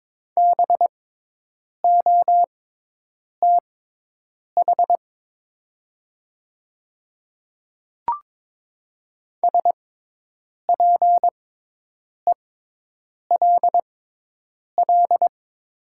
Morse code, heard as a keyed tone, about 22 wpm, spaced as for 8 wpm, 700 Hz. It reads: BOTH SPELL
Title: 6x Character Spacing - Top 300 Words - 22wpm